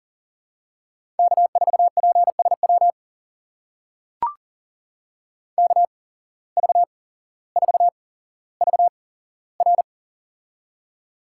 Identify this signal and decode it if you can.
Morse code, heard as a keyed tone, about 40 wpm, spaced as for 17 wpm, 700 Hz. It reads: K4PSW KV4VR
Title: USA Call Signs - 5 Characters - 8x Character Spacing - 40wpm